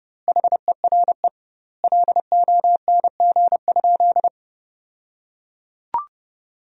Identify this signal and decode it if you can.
Morse code, heard as a keyed tone, about 30 wpm, 700 Hz. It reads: HERE LONG?
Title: 2x Word-Spacing - Sentences from Top 200 Words - 30wpm